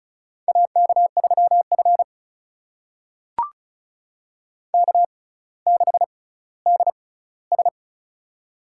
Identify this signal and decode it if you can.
Morse code, heard as a keyed tone, about 36 wpm, spaced as for 18 wpm, 700 Hz. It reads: AK3F K6DS
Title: USA Call Signs - 4 Characters - 6x Character Spacing - 35wpm